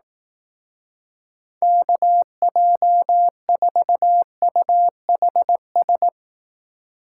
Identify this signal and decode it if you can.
Morse code, heard as a keyed tone, about 18 wpm, 700 Hz. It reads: KJ4UHS
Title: Speed Racing - USA 2x3 Call Signs 15wpm